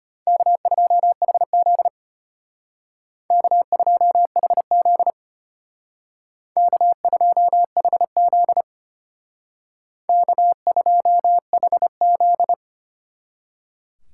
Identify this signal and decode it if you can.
Morse code, heard as a keyed tone, about 32 wpm, 700 Hz. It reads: K2HZ K2HZ K2HZ K2HZ